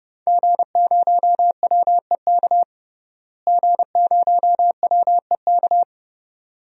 Morse code, heard as a keyed tone, about 30 wpm, 700 Hz. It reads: G0WEK G0WEK
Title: DXCC-100 Call Signs - 5 Characters - Repeated 2x - 30wpm